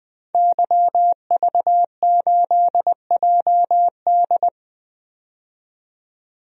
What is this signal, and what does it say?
Morse code, heard as a keyed tone, about 20 wpm, 700 Hz. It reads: YV8JD
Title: DXCC-100 Call Signs - 5 Characters - 20wpm